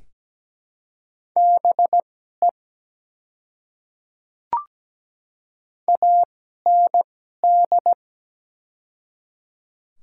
Morse code, heard as a keyed tone, about 17 wpm, spaced as for 12 wpm, 700 Hz. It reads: BE AND
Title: 2x Character Spacing - Top 100 Words - 17wpm